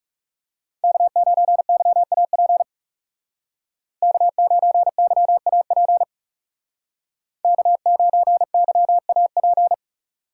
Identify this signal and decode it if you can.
Morse code, heard as a keyed tone, about 39 wpm, 700 Hz. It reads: K9YAP K9YAP K9YAP